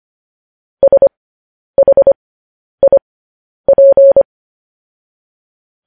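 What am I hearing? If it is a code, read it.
Morse code, heard as a keyed tone, about 25 wpm, spaced as for 10 wpm, 550 Hz. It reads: SHIP
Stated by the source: Morse news headlines practice at 10.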